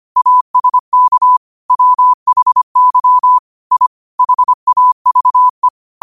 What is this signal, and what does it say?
Morse code, heard as a keyed tone, about 25 wpm, 1000 Hz. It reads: ASK WHY I HAVE
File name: qotdfiles / 25wpm